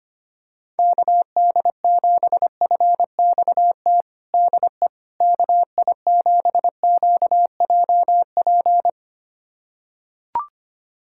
Morse code, heard as a keyed tone, about 25 wpm, 700 Hz. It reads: KD7FXT DE KI7QJP
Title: USA Callsign DE Callsign - 6 Characters - 25wpm